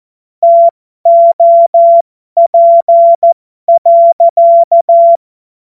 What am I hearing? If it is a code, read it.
Morse code, heard as a keyed tone, about 14 wpm, spaced as for 12 wpm, 680 Hz. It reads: TOP.
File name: Morsekurs / Witze